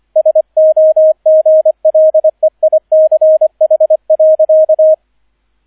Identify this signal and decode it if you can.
Morse code, heard as a keyed tone, about 25 wpm, 630 Hz, with radio noise. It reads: SOGLEICH.